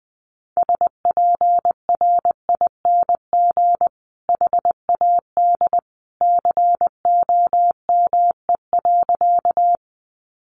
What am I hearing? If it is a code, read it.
Morse code, heard as a keyed tone, about 20 wpm, 700 Hz. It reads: SPRING HAD COME.
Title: Sentences from Top 1200 Words 20wpm